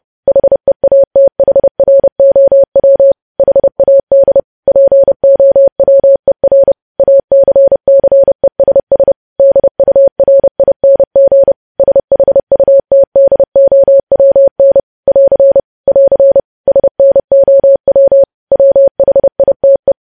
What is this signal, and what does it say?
Morse code, heard as a keyed tone, about 30 wpm, 550 Hz. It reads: HEATHROW HAD POWER ACCESS DURING SHUTDOWN <AR> <AR> SNOW WHITE
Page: news headlines in morse code at 30 wpm